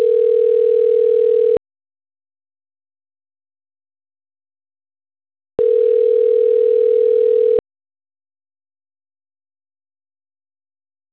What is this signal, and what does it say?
Ringback tone (North America), 440 + 480 Hz.